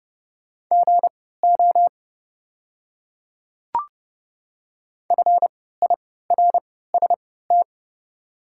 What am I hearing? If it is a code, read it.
Morse code, heard as a keyed tone, about 29 wpm, spaced as for 17 wpm, 700 Hz. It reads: GO FIRST
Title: 3x Character Spacing - Top 200 Words - 30wpm